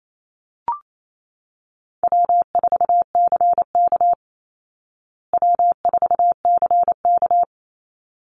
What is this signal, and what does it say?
Morse code, heard as a keyed tone, about 28 wpm, 700 Hz. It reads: W4CK W4CK
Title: Easy Call Signs Repeated 2x - 28wpm